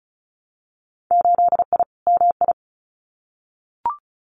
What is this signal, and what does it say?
Morse code, heard as a keyed tone, about 36 wpm, 700 Hz. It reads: 8I MI
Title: Winter Field Day Exchanges Encore 35wpm